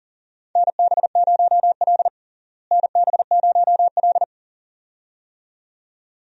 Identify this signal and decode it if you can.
Morse code, heard as a keyed tone, about 40 wpm, 700 Hz. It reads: NB0L NB0L
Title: USA Call Signs - 4 Characters - Repeated 2x - 40wpm